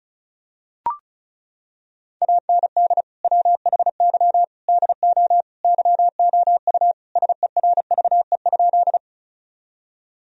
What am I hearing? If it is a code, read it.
Morse code, heard as a keyed tone, about 36 wpm, 700 Hz. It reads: AND WHY DO YOU SERVE?
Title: Sentences from Top 1000 Words - Review Part 2 - 35wpm